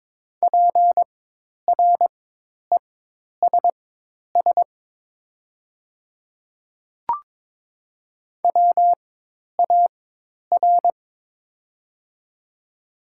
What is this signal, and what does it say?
Morse code, heard as a keyed tone, about 22 wpm, spaced as for 10 wpm, 700 Hz. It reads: PRESS WAR
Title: 4x Character Spacing - Top 500 Words - 22wpm